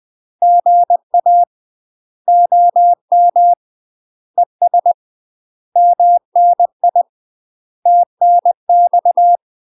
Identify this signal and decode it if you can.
Morse code, heard as a keyed tone, about 20 wpm, 700 Hz. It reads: GA OM ES MNI TNX
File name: UK MASTER QSOs TRAINING / 20WPM / UK MASTER QSOs 0047